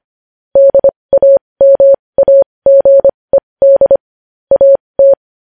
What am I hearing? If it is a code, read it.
Morse code, heard as a keyed tone, about 25 wpm, spaced as for 20 wpm, 550 Hz. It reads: DAMAGED AT
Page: News Headlines in Morse Code at 20 WPM